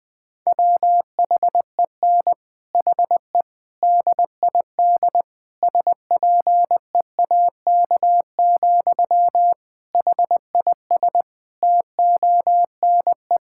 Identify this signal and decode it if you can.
Morse code, heard as a keyed tone, about 20 wpm, 700 Hz. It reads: WHEN HE DID SPEAK, HIS TONE
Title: Sentences from Top 1600 Words 20wpm